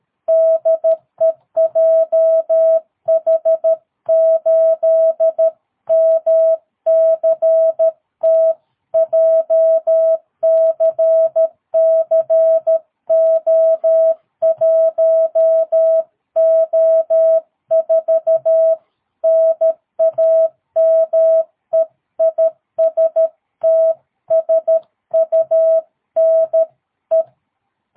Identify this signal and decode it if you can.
Morse code, heard as a keyed tone, about 13 wpm, 660 Hz. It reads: DE JH8MCT JCCO1O4 NAME IS TSUNE
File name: jh8cbh / 2022cw